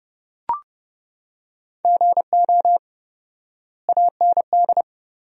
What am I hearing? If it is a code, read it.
Morse code, heard as a keyed tone, about 30 wpm, 700 Hz. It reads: GO AND
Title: 4x Word-Spacing - Sentences from Top 200 Words - 30wpm